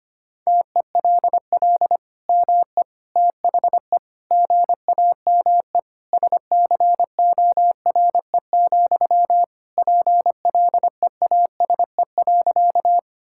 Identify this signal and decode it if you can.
Morse code, heard as a keyed tone, about 25 wpm, 700 Hz. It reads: TELL ME THE GAME SCORE, PLEASE.
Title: Sentences from Top 1600 Words 25wpm